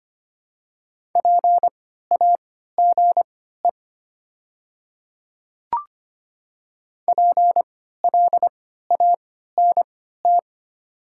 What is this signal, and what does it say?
Morse code, heard as a keyed tone, about 25 wpm, spaced as for 14 wpm, 700 Hz. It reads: PAGE PLANT
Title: 3x Character Spacing - Top 500 Words - 25wpm